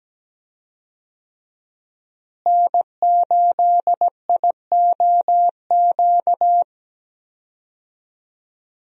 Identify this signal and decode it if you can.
Morse code, heard as a keyed tone, about 17 wpm, 700 Hz. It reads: N8IOQ
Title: USA Call Signs - 5 Characters - Repeated 2x - 17wpm